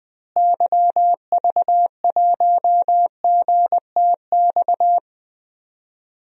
Morse code, heard as a keyed tone, about 20 wpm, 700 Hz. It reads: YV1GTX
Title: DXCC-100 Call Signs - 6 Characters - Repeated 2x - 20wpm